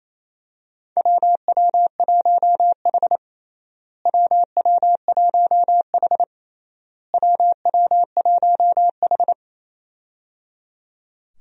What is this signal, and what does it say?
Morse code, heard as a keyed tone, about 28 wpm, 700 Hz. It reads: WW1H WW1H WW1H
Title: USA Call Signs - 4 Characters - Repeated 3x - 28wpm